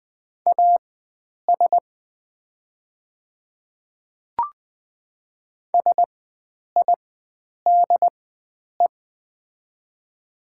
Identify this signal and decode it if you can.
Morse code, heard as a keyed tone, about 20 wpm, spaced as for 9 wpm, 700 Hz. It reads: AS SIDE
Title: 4x Character Spacing - Top 300 Words - 20wpm